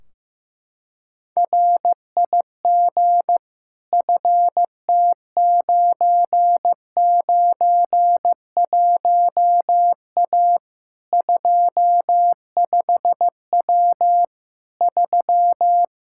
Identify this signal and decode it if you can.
Morse code, heard as a keyed tone, about 16 wpm, 700 Hz. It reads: RIG FT991A 25W 3
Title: K1USN SST QSO Practice 15wpm